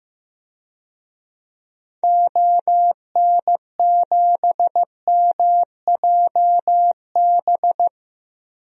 Morse code, heard as a keyed tone, about 15 wpm, 700 Hz. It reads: ON7MJB
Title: DXCC-100 Call Signs - 6 Characters - Repeated 3x - 15wpm